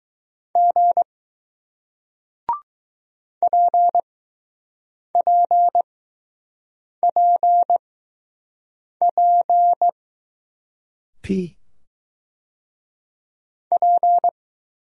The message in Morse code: G PPPP P

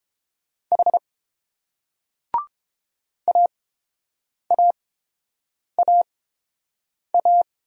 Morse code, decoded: H AAAA

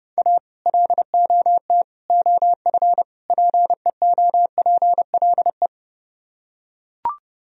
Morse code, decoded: A LOT OF PEOPLE

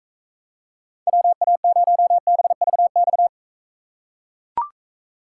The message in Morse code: WA0BVX

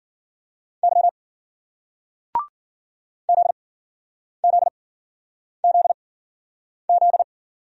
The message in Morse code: Y ZZZZ

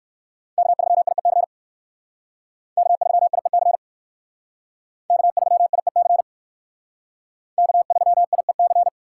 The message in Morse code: K3IEC K3IEC K3IEC K3IEC

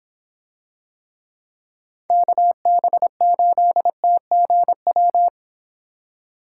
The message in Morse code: KB8TGW

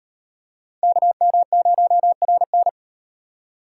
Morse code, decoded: KM0RN

